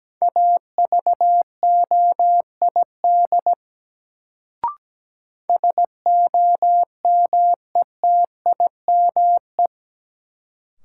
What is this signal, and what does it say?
Morse code, heard as a keyed tone, about 17 wpm, 700 Hz. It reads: AVOID SOMETIME
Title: Sentences from Top 1000 Words - Review Part 6 - New Word Drill - 17wpm